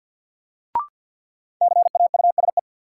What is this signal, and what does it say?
Morse code, heard as a keyed tone, about 53 wpm, 700 Hz. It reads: CAUSE
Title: Sentences from Top 200 Words - New Word Drill - 50wpm